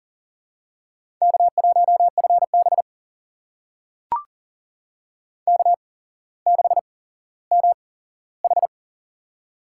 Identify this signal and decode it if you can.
Morse code, heard as a keyed tone, about 42 wpm, spaced as for 17 wpm, 700 Hz. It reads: K1FB K6MH
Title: USA Call Signs - 4 Characters - 8x Character Spacing - 40wpm